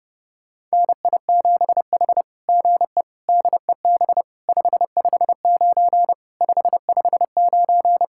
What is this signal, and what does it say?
Morse code, heard as a keyed tone, about 30 wpm, 700 Hz. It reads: NI7H GE DEB 559 559
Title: SOTA QSO Practice 30wpm